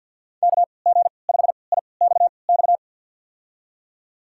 Morse code, heard as a keyed tone, about 51 wpm, spaced as for 28 wpm, 690 Hz. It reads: KG5IXX
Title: USA Call Signs - 6 Characters - 3x Character Spacing - 50wpm